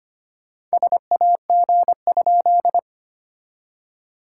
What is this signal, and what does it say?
Morse code, heard as a keyed tone, about 25 wpm, 690 Hz. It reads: SAG?
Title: SOTA QSO Practice 25wpm